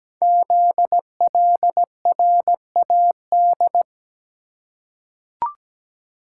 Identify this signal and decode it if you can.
Morse code, heard as a keyed tone, about 17 wpm, 700 Hz. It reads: ZLRAD
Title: Code Groups - 5 Letter:Numbers - 17wpm